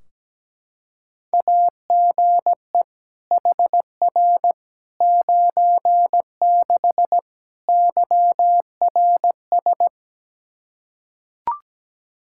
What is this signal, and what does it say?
Morse code, heard as a keyed tone, about 17 wpm, 700 Hz. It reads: AGE HR 96 YRS
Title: K1USN SST QSO Practice 17wpm